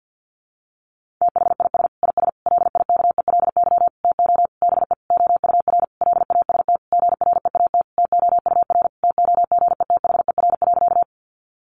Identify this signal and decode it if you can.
Morse code, heard as a keyed tone, about 50 wpm, 700 Hz. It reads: THIS IS LIKELY TO BE OUR LAST GREAT TOUR TOGETHER.
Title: Sentences from Top 1400 Words 50wpm